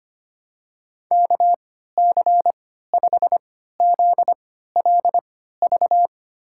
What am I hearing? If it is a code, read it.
Morse code, heard as a keyed tone, about 25 wpm, spaced as for 14 wpm, 700 Hz. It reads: KC5ZLV
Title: USA Call Signs - 6 Characters - 3x Character Spacing - 25wpm